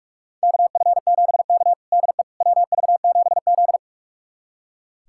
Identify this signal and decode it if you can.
Morse code, heard as a keyed tone, about 44 wpm, 690 Hz. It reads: KF7K DE WV7Z
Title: USA Callsign DE Callsign - 4 Characters - 45wpm